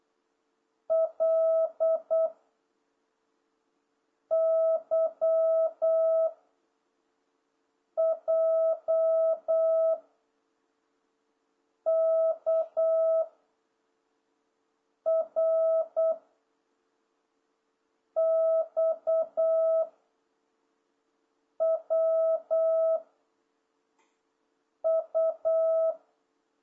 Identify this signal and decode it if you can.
Morse code, heard as a keyed tone, about 8 wpm, 660 Hz. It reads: L Y J K R X W U